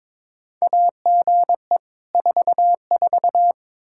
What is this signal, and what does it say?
Morse code, heard as a keyed tone, about 22 wpm, 700 Hz. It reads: AGE 44
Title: Practice QSO - Exchange 3 - More Info - 22wpm